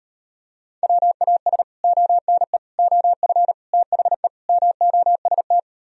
Morse code, heard as a keyed tone, about 38 wpm, 700 Hz. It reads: WAS ONE OF THE MOST